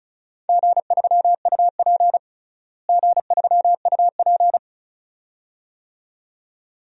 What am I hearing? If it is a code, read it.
Morse code, heard as a keyed tone, about 34 wpm, 700 Hz. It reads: G3UP G3UP
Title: DXCC-100 Call Signs - 4 Characters - Repeated 2x - 35wpm